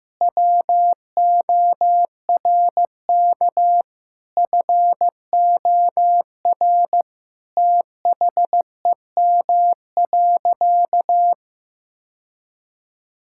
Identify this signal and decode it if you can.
Morse code, heard as a keyed tone, about 16 wpm, 700 Hz. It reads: WORK FOR THEM.
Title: Sentences from Top 1000 Words - Review Part 5 - 15wpm